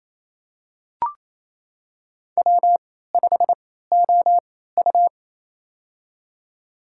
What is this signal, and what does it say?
Morse code, heard as a keyed tone, about 28 wpm, spaced as for 16 wpm, 700 Hz. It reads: W5OU
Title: USA Call Signs - 3x Character Spacing - 28wpm